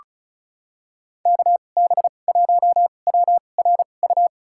Morse code, heard as a keyed tone, about 34 wpm, spaced as for 25 wpm, 700 Hz. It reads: KB1WRU
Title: USA Call Signs - 6 Characters - 2x Character Spacing - 35wpm